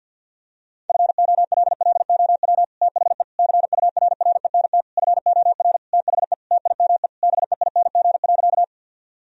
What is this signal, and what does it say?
Morse code, heard as a keyed tone, about 50 wpm, 700 Hz. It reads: FOLLOW THE CURRENT FOR THE TIME BEING.
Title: Sentences from Top 1000 Words - Review Part 4 - 50wpm